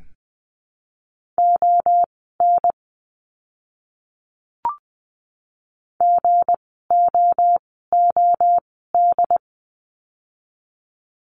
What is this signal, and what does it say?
Morse code, heard as a keyed tone, about 20 wpm, spaced as for 14 wpm, 700 Hz. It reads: ON GOOD
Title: 206 Lesson 33f - Letter j - Words 20wpm